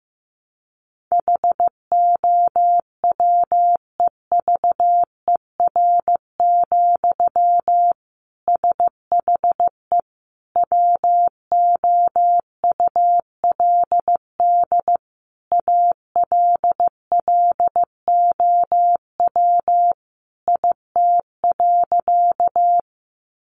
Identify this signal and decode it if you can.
Morse code, heard as a keyed tone, about 15 wpm, 700 Hz. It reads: HOWEVER, SHE WOULD ALLOW IT.
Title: Sentences from Top 700 Words 15wpm